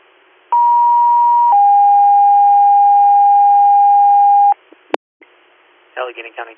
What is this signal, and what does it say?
Two-tone page, heard as a one-second tone, then a three-second tone, about 950 then 810 Hz, with speech. It is heard from 0.5 to 4.5 s.